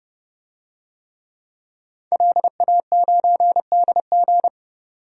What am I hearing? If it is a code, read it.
Morse code, heard as a keyed tone, about 31 wpm, 700 Hz. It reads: LA9DG
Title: DXCC-100 Call Signs - 5 Characters - 30wpm